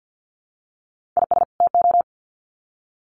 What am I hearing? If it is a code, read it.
Morse code, heard as a keyed tone, about 50 wpm, 700 Hz. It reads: IS TO
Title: Sets of 2 Words 50wpm